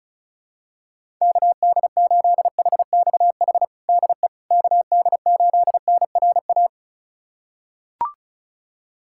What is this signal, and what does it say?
Morse code, heard as a keyed tone, about 36 wpm, 700 Hz. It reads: KD8HXH DE KD8NRA
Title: USA Callsign DE Callsign 35wpm